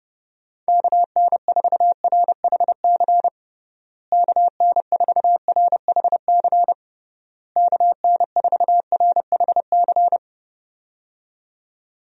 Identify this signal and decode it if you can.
Morse code, heard as a keyed tone, about 32 wpm, 700 Hz. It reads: KN4RHC KN4RHC KN4RHC
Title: USA Call Signs - 6 Characters - Repeated 3x - 30wpm